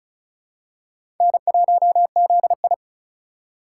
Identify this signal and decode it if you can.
Morse code, heard as a keyed tone, about 34 wpm, 700 Hz. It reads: N1ZI
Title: USA Call Signs - 4 Characters - 4x Character Spacing - 35wpm